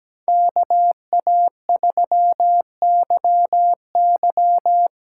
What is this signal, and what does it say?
Morse code, heard as a keyed tone, about 17 wpm, 700 Hz. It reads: KA3YY
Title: USA Call Signs - 6 Characters - 8x Character Spacing - 17wpm